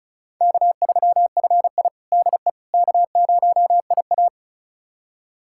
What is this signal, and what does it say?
Morse code, heard as a keyed tone, about 34 wpm, 700 Hz. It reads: K3FI DE K0IA
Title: USA Callsign DE Callsign - 4 Characters - 35wpm